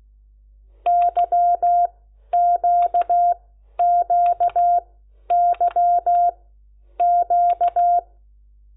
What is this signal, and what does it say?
Morse code, heard as a keyed tone, about 16 wpm, spaced as for 10 wpm, 700 Hz. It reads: YQQYQ
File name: Lecciones / CLASE Nº10